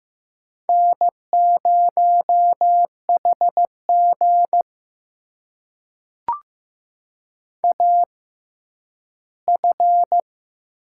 Morse code, heard as a keyed tone, about 15 wpm, spaced as for 10 wpm, 700 Hz. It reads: N0HG AF
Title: USA Call Signs - 4 Characters - 6x Character Spacing - 15wpm